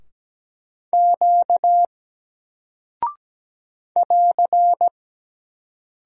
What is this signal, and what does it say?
Morse code, heard as a keyed tone, about 17 wpm, 700 Hz. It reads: Q <AR>
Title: Single Character (Letter:Number:Punctuation:Prosign) Alt 17wpm